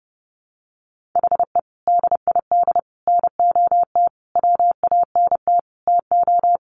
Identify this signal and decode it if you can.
Morse code, heard as a keyed tone, about 28 wpm, 700 Hz. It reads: HE DID NOT WANT TO
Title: Sentences from Top 1200 Words 30wpm